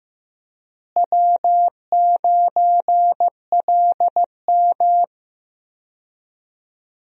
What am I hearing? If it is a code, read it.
Morse code, heard as a keyed tone, about 15 wpm, 700 Hz. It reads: W9LM
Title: USA Call Signs - 4 Characters - Repeated 2x - 15wpm